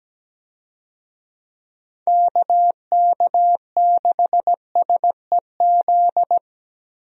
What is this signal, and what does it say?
Morse code, heard as a keyed tone, about 17 wpm, 700 Hz. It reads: KK6SEZ
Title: USA Call Signs - 6 Characters - Repeated 2x - 17wpm